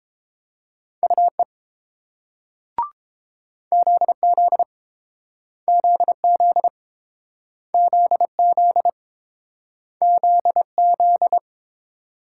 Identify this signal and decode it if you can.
Morse code, heard as a keyed tone, about 30 wpm, 700 Hz. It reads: UE ZZ ZZ ZZ ZZ